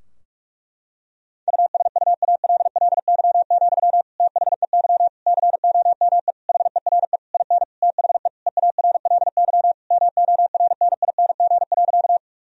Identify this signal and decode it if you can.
Morse code, heard as a keyed tone, about 46 wpm, 700 Hz. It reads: USUALLY, THEY COME HERE IN THE EARLY MORNING.
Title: Sentences from Top 1000 Words - Review Part 3 - 45wpm